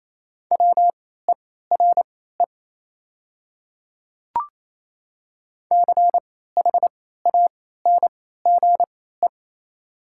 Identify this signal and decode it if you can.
Morse code, heard as a keyed tone, about 27 wpm, spaced as for 16 wpm, 700 Hz. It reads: WERE CHANGE
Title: 3x Character Spacing - Top 500 Words - 28wpm